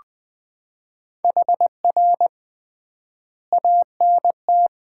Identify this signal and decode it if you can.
Morse code, heard as a keyed tone, about 21 wpm, 700 Hz. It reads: HR ANT
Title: Common QSO Phrases - 3x Word-Spacing - 20wpm